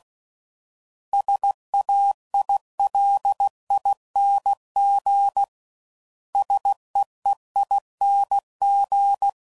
Morse code, heard as a keyed tone, about 16 wpm, 800 Hz. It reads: SAILING SEEING
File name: Suffixes / Special INGs / Sngl Ings